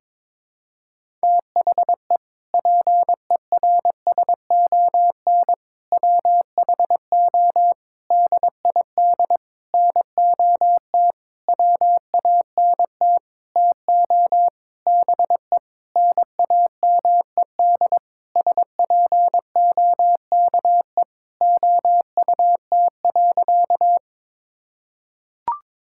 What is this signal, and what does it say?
Morse code, heard as a keyed tone, about 22 wpm, 700 Hz. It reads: THE PERSON WHO DID NOT WANT TO BE NAMED SPOKE OUT.
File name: Sentences with 7-Word Phrases - 22wpm